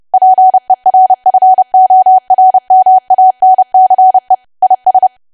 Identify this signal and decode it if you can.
Morse code, heard as a keyed tone, about 30 wpm, 750 Hz. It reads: PERFORMANCE IS
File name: cwa / W1AW-Level3 / 30WPM